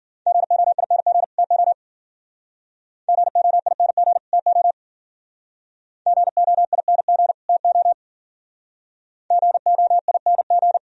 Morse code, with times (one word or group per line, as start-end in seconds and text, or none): 0.3-1.2 GOING
1.4-1.7 TO
3.1-4.2 GOING
4.3-4.7 TO
6.1-7.3 GOING
7.5-7.9 TO
9.3-10.8 GOING